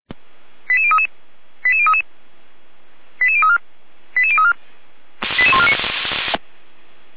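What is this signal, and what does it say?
Five-tone selective call (ZVEI), heard as quick runs of five tones, tones of about 70 ms. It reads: 88033 88033 88034 88034 88149
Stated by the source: ZVEI five-tone selective call received via SDR.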